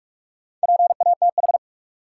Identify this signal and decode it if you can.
Morse code, heard as a keyed tone, about 46 wpm, 700 Hz. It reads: PATH
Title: Sentences from Top 1400 Words - New Word Drill - 45wpm